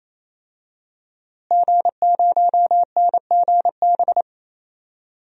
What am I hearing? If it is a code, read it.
Morse code, heard as a keyed tone, about 27 wpm, 700 Hz. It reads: G0NGB
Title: DXCC-100 Call Signs - 5 Characters - Repeated 3x - 28wpm